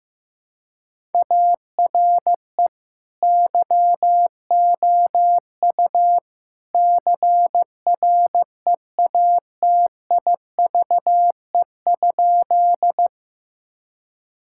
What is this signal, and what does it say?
Morse code, heard as a keyed tone, about 15 wpm, 700 Hz. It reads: ARE YOU CREATIVE?
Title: Sentences from Top 1000 Words - Review Part 5 - 15wpm